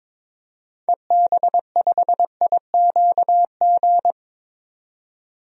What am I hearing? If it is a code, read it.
Morse code, heard as a keyed tone, about 22 wpm, 700 Hz. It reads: EB5IQG
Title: DXCC-100 Call Signs - 6 Characters - 22wpm